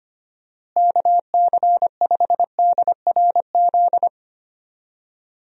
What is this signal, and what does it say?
Morse code, heard as a keyed tone, about 25 wpm, 700 Hz. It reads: KC5DRZ
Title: USA Call Signs - 6 Characters - Repeated 2x - 25wpm